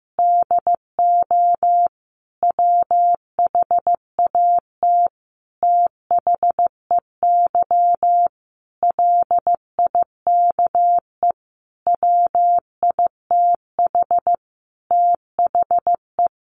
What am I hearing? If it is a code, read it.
Morse code, heard as a keyed tone, about 15 wpm, 700 Hz. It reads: DO WHAT THEY LIKE WITH THE
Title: Sentences from Top 1200 Words 15wpm